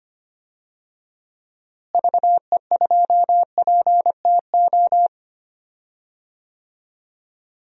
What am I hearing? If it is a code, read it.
Morse code, heard as a keyed tone, about 25 wpm, 700 Hz. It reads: VE2PTO
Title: DXCC-100 Call Signs - 6 Characters - Repeated 3x - 25wpm